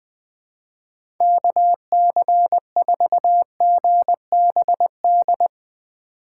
Morse code, heard as a keyed tone, about 20 wpm, 700 Hz. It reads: KC4GBD
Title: USA Call Signs - 2x Character Spacing - 20wpm